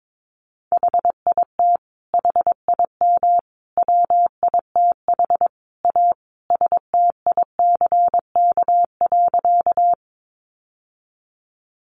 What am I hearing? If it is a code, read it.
Morse code, heard as a keyed tone, about 22 wpm, 700 Hz. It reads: HIT HIM WITH A STICK.
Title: Sentences from Top 1200 Words 22wpm